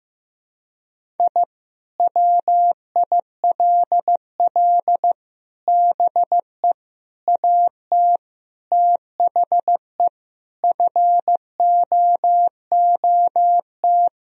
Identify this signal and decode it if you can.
Morse code, heard as a keyed tone, about 15 wpm, 700 Hz. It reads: I WILL BE AT THE FOOT
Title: Sentences from Top 1000 Words - Review Part 3 - 15wpm